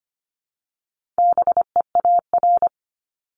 Morse code, heard as a keyed tone, about 25 wpm, 700 Hz. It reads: BEAR
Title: ING Suffix - Top 100 Words with Base Words 25wpm